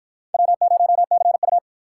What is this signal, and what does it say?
Morse code, heard as a keyed tone, about 53 wpm, 700 Hz. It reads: W0CU